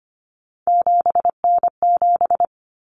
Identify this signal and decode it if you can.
Morse code, heard as a keyed tone, about 25 wpm, 700 Hz. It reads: 7N7
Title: Common Cut Numbers - 3-Digit 25wpm